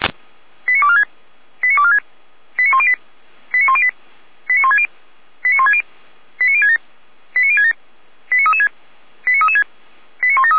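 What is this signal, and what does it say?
Five-tone selective call (ZVEI), heard as quick runs of five tones, tones of about 70 ms. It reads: 89246 89246 89108 89108 89160 89160 89067 89067 89336 89336 89173
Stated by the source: ZVEI five-tone selective call on 455, received by SDR.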